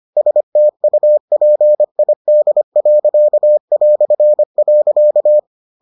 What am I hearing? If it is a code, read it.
Morse code, heard as a keyed tone, about 25 wpm, 600 Hz. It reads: STUPID.".